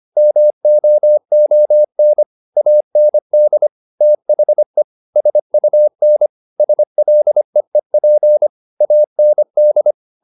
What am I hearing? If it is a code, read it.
Morse code, heard as a keyed tone, about 25 wpm, 600 Hz. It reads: MOON AND THE SUN SLEEP AND